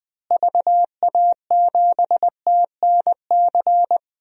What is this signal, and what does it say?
Morse code, heard as a keyed tone, about 20 wpm, 700 Hz. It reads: VA7TNC
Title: DXCC-100 Call Signs - 6 Characters - Repeated 3x - 20wpm